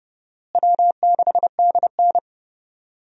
Morse code, heard as a keyed tone, about 30 wpm, 700 Hz. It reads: W6DN
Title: USA Call Signs - 4 Characters - Repeated 2x - 30wpm